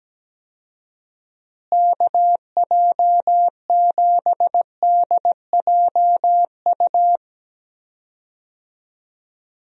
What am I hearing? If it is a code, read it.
Morse code, heard as a keyed tone, about 17 wpm, 700 Hz. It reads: KJ7DJU
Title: USA Call Signs - 6 Characters - Repeated 3x - 17wpm